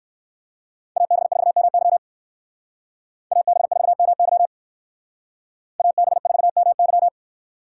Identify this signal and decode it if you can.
Morse code, heard as a keyed tone, about 58 wpm, 700 Hz. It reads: AB4MY AB4MY AB4MY